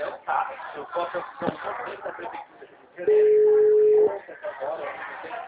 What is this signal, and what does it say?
Ringback tone (Europe, 425 Hz), with speech.